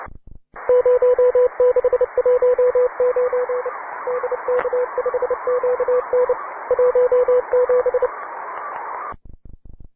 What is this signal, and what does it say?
Morse code, heard as a keyed tone, about 29 wpm, 500 Hz, with radio noise. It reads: 0619 DK5QN 17